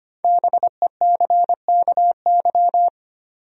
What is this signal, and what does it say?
Morse code, heard as a keyed tone, about 25 wpm, 700 Hz. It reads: BECKY
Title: USA Common Names - 25wpm